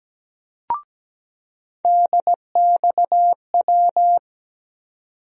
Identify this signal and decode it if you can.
Morse code, heard as a keyed tone, about 17 wpm, 700 Hz. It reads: DXW
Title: USA Call Sign Postfixes - 3 Letters 17wpm